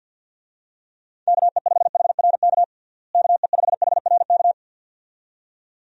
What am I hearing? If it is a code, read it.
Morse code, heard as a keyed tone, about 46 wpm, 700 Hz. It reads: KE5HRK KE5HRK
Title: USA Call Signs - 6 Characters - Repeated 2x - 50wpm